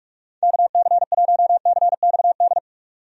Morse code, heard as a keyed tone, about 46 wpm, 700 Hz. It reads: KC1CXD
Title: USA Call Signs - 6 Characters - 45wpm